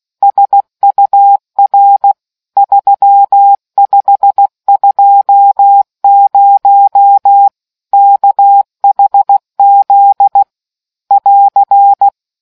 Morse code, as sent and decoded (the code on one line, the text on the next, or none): ... ..- .-. / ...-- ..... ..--- ----- / -.- .... --.. / .-.-.
SUR 3520 KHZ <AR>